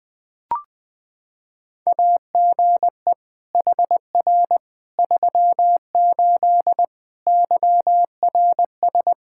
Morse code, heard as a keyed tone, about 20 wpm, 700 Hz. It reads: AGE HR 38 YRS
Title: K1USN SST QSO Practice 20wpm